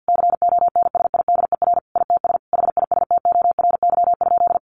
Morse code, heard as a keyed tone, about 50 wpm, 700 Hz. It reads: CONSIDER ITS HISTORY?
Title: Sentences from Top 1100 Words 50wpm